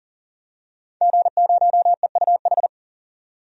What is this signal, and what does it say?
Morse code, heard as a keyed tone, about 40 wpm, 700 Hz. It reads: G0EUH
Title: DXCC-100 Call Signs - 5 Characters - 40wpm